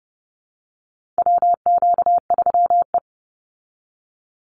Morse code, heard as a keyed tone, about 30 wpm, 700 Hz. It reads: WQ3E
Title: Call Signs Repeated 2x - 30wpm